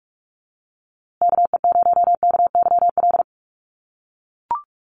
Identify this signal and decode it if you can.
Morse code, heard as a keyed tone, about 44 wpm, 700 Hz. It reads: KE0KYL
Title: USA Call Signs Repeated 3x 45wpm